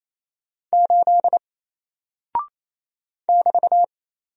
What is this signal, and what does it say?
Morse code, heard as a keyed tone, about 28 wpm, 700 Hz. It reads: 8 <BT>
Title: Single Character (Letter:Number:Punctuation:Prosign) Alt 28wpm